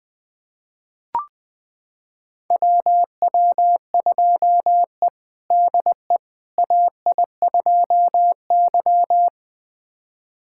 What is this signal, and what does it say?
Morse code, heard as a keyed tone, about 21 wpm, 700 Hz. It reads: WW2E DE AI2Y
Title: USA Callsign DE Callsign - 4 Characters - 20wpm